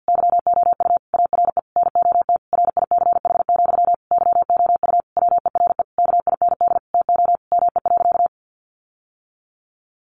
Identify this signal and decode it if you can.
Morse code, heard as a keyed tone, about 50 wpm, 700 Hz. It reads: YOU ARE NOT RICH, YOU WERE KIND TO ME.
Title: Sentences from Top 1100 Words 50wpm